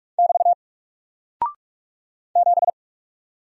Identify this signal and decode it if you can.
Morse code, heard as a keyed tone, about 46 wpm, 700 Hz. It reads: <BT> 7
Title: Single Character (Letter:Number:Punctuation:Prosign) Alt 45wpm